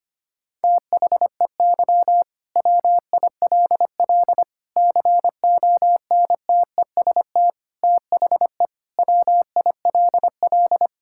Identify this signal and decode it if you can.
Morse code, heard as a keyed tone, about 25 wpm, 700 Hz. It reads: THEY WILL CONTEST THE WILL
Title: Sentences from Top 1600 Words 25wpm